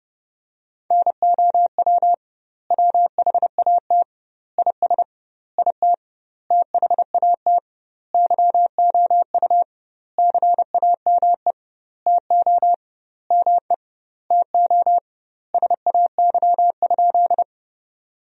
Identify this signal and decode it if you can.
Morse code, heard as a keyed tone, about 30 wpm, 700 Hz. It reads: NOW WHAT IS IT THAT YOU CAME TO ME TO SAY?
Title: 2x Word-Spacing - Sentences from Top 200 Words - 30wpm